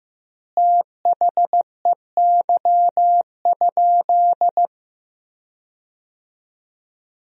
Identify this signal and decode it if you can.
Morse code, heard as a keyed tone, about 15 wpm, 700 Hz. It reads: THEY?
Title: 4x Word-Spacing - Sentences from Top 300 Words - 15wpm